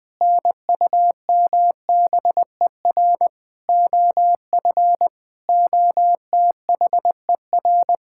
Morse code, heard as a keyed tone, about 20 wpm, 700 Hz. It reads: NUMBER OF OTHER
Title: Sentences with 7-Word Phrases - 20wpm